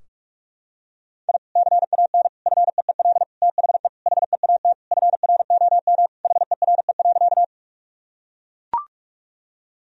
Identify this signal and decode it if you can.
Morse code, heard as a keyed tone, about 41 wpm, 700 Hz. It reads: I CAN FEEL THE HEAT FROM HERE.
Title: Sentences from Top 1000 Words - Review Part 3 - 45wpm